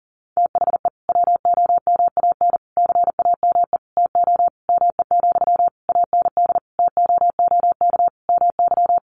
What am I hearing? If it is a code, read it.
Morse code, heard as a keyed tone, about 40 wpm, 700 Hz. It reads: THE WOMAN CAME TO ME, AND TOOK MY
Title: Sentences from Top 1100 Words 40wpm